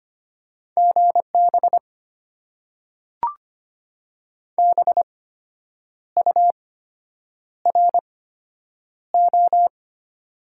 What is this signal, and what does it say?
Morse code, heard as a keyed tone, about 25 wpm, spaced as for 8 wpm, 690 Hz. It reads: GB BURO
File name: Top 90 QSO Elements - 8x Character Spacing - 25wpm